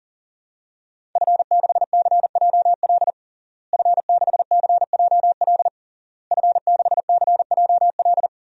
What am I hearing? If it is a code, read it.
Morse code, heard as a keyed tone, about 40 wpm, 700 Hz. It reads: F6CJL F6CJL F6CJL
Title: DXCC-100 Call Signs - 5 Characters - Repeated 3x - 40wpm